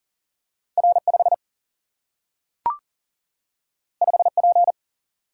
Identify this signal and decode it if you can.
Morse code, heard as a keyed tone, about 39 wpm, 700 Hz. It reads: R 5 5 P
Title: Single-Letter Single-Number 40wpm